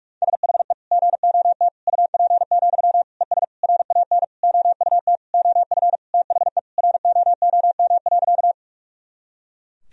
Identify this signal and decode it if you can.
Morse code, heard as a keyed tone, about 46 wpm, 700 Hz. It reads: SHE GOT UP, ES RAN OUT OF THE ROOM.
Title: Sentences from Top 500 Words and Abbreviations 45wpm